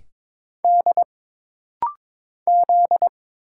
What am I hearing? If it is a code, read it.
Morse code, heard as a keyed tone, about 22 wpm, 700 Hz. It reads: D Z
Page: This rapid-fire timing is geared to push your ICR (Instant Character Recognition) skills.